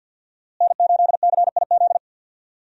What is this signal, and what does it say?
Morse code, heard as a keyed tone, about 50 wpm, 700 Hz. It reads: N8CIZ
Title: USA Call Signs - 5 Characters - Repeated 2x - 50wpm